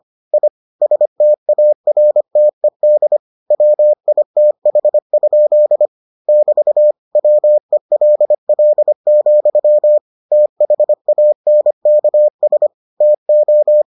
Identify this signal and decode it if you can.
Morse code, heard as a keyed tone, about 25 wpm, 600 Hz. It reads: I STARTED WITH? <BT> WELL, THANKS TO